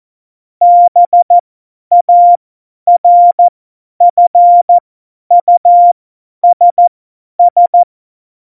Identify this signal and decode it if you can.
Morse code, heard as a keyed tone, about 14 wpm, spaced as for 10 wpm, 700 Hz. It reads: BARFUSS
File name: Morsekurs / Witze / Witz 05 Mai 2020, 50 F70